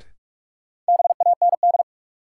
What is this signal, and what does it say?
Morse code, heard as a keyed tone, about 41 wpm, 700 Hz. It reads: BAND
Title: Sentences from Top 1600 Words - New Word Drill - 45wpm